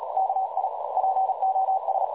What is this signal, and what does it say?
Morse code, heard as a keyed tone, about 36 wpm, 760 Hz, with noise. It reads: KI POS
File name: dlazaawansowanych / nagraniazpasma / websdr recording 3565.0kHz